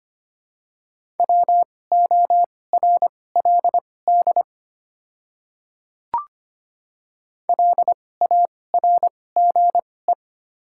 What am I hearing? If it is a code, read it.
Morse code, heard as a keyed tone, about 25 wpm, spaced as for 18 wpm, 700 Hz. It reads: WORLD LARGE
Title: 2x Character Spacing - Top 300 Words - 25wpm